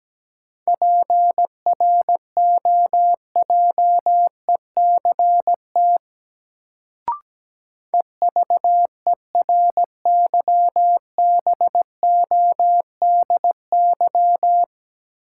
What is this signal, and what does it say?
Morse code, heard as a keyed tone, about 17 wpm, 700 Hz. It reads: PROJECT EVERYBODY
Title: Sentences from Top 1400 Words - New Word Drill Alt - 17wpm